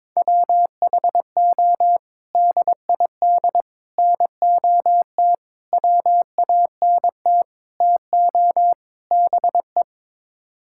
Morse code, heard as a keyed tone, about 22 wpm, 700 Hz. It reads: WHO DID NOT WANT TO BE
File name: practice / 22wpm / Sets of 6 Words - 22wpm